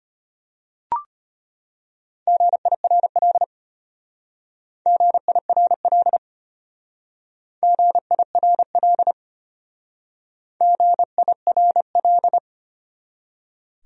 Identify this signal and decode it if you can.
Morse code, heard as a keyed tone, about 33 wpm, 700 Hz. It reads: GIRL GIRL GIRL GIRL